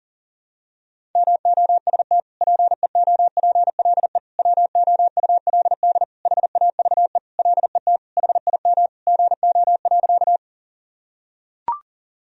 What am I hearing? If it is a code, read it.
Morse code, heard as a keyed tone, about 40 wpm, 700 Hz. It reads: MOST PEOPLE WOULD HAVE LET HIM GO.
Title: Sentences from Top 1000 Words - Review Part 1 - 40wpm